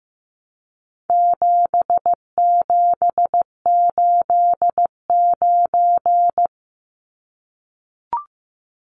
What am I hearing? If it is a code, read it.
Morse code, heard as a keyed tone, about 15 wpm, 700 Hz. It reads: 7789
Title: Numbers - 4-Digit 15wpm